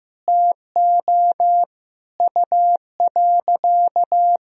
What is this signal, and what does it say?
Morse code, heard as a keyed tone, about 14 wpm, 700 Hz. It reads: TO U.
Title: Sentences from Top 200 Words and Abbreviations 15wpm